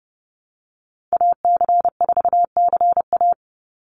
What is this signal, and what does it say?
Morse code, heard as a keyed tone, about 28 wpm, 700 Hz. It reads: AC4CA
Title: Call Signs Repeated 3x - 30wpm